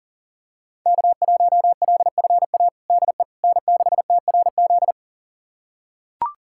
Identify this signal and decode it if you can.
Morse code, heard as a keyed tone, about 40 wpm, 700 Hz. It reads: K1LFA DE N6TRZ
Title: USA Callsign DE Callsign - 5 Characters - 40wpm